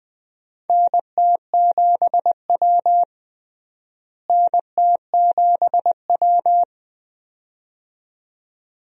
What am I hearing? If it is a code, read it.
Morse code, heard as a keyed tone, about 20 wpm, 700 Hz. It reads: NT7W NT7W
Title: USA Call Signs - 4 Characters - Repeated 2x - 20wpm